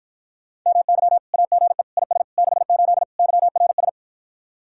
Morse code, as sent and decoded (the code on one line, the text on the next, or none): -- -.-- / .- --. / .. ... / -.... ---.. / -.-- .-. ...
MY AGE IS 68 YRS